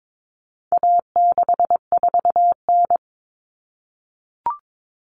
Morse code, heard as a keyed tone, about 22 wpm, 700 Hz. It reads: A64N
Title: Common Cut Numbers - 4-Digit 22wpm